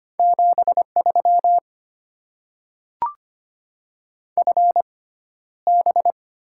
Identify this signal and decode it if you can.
Morse code, heard as a keyed tone, about 25 wpm, spaced as for 13 wpm, 700 Hz. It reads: 73 FB
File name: Top 20 QSO Elements - 6x Character Spacing - 25wpm